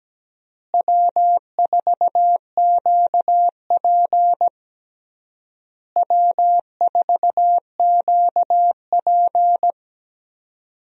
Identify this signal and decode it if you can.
Morse code, heard as a keyed tone, about 17 wpm, 700 Hz. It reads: W4QP W4QP
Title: USA Call Signs - 4 Characters - Repeated 3x - 17wpm